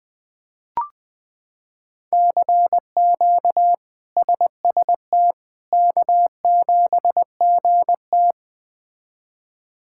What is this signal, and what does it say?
Morse code, heard as a keyed tone, about 21 wpm, 700 Hz. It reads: CQ SST K7GT
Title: K1USN SST QSO Practice - 20wpm